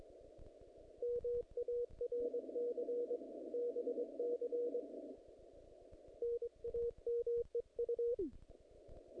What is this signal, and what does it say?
Morse code, heard as a keyed tone, about 22 wpm, 470 Hz. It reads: MARC BC NAMEU
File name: contests / naqp-cw-k5tr-op-k5pi- -0100